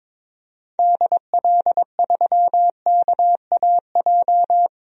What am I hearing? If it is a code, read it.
Morse code, heard as a keyed tone, about 22 wpm, 700 Hz. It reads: DL3KAJ
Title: DXCC-100 Call Signs - 6 Characters - 22wpm